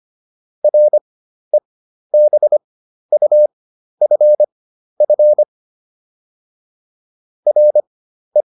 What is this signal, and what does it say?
Morse code, heard as a keyed tone, about 25 wpm, spaced as for 12 wpm, 600 Hz. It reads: REBUFF RE